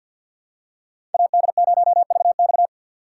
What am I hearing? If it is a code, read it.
Morse code, heard as a keyed tone, about 52 wpm, 700 Hz. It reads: AD0VX
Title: USA Call Signs - 5 Characters - 4x Character Spacing - 50wpm